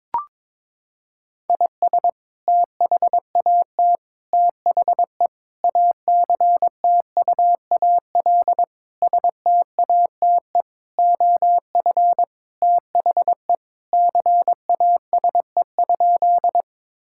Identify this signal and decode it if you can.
Morse code, heard as a keyed tone, about 22 wpm, 700 Hz. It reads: IS THAT THE ACTUAL STATE OF THE CASE?
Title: Sentences from Top 1000 Words - Review Part 5 - 22wpm